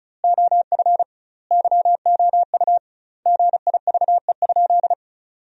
Morse code, heard as a keyed tone, about 34 wpm, 700 Hz. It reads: OF YOU GIVE?
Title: 2x Word-Spacing - Sentences from Top 200 Words - 35wpm